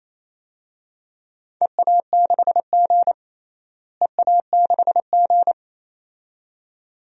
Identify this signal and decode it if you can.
Morse code, heard as a keyed tone, about 28 wpm, 700 Hz. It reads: EA6G EA6G
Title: DXCC-100 Call Signs - 4 Characters - Repeated 2x - 28wpm